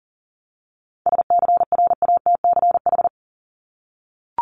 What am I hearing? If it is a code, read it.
Morse code, heard as a keyed tone, about 40 wpm, 700 Hz. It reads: SCRATCH
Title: Top 1501-1750 Words 40wpm